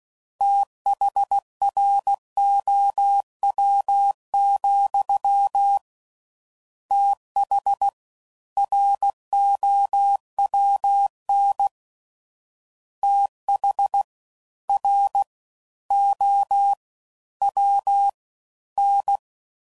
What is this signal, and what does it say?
Morse code, heard as a keyed tone, about 16 wpm, 800 Hz. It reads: THROW, TH ROWN TH R O W N
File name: Prefixes / FLs & THs / Fast TH